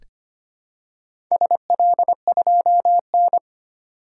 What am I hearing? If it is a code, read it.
Morse code, heard as a keyed tone, about 25 wpm, 700 Hz. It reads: SL2N
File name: practice / 25wpm / Four Letter-Number Combinations - 25wpm